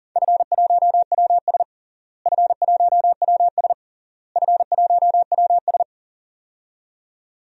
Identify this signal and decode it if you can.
Morse code, heard as a keyed tone, about 40 wpm, 700 Hz. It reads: F1WS F1WS F1WS
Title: DXCC-100 Call Signs - 4 Characters - Repeated 3x - 40wpm